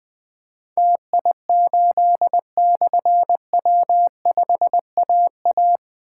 Morse code, heard as a keyed tone, about 20 wpm, 700 Hz. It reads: TI8/W5AA
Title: Difficult Call Signs 20wpm